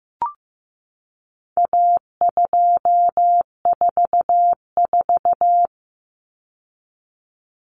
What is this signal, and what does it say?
Morse code, heard as a keyed tone, about 15 wpm, 700 Hz. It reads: A244
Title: Common Cut Numbers - 4-Digit 15wpm